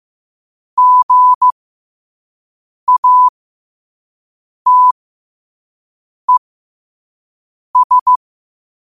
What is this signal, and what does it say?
Morse code, heard as a keyed tone, about 15 wpm, spaced as for 5 wpm, 1000 Hz. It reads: GATES
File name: qotdfiles / 5wpm